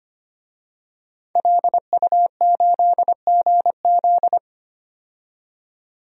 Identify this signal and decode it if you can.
Morse code, heard as a keyed tone, about 25 wpm, 700 Hz. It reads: LU8GZ
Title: DXCC-100 Call Signs - 5 Characters - 25wpm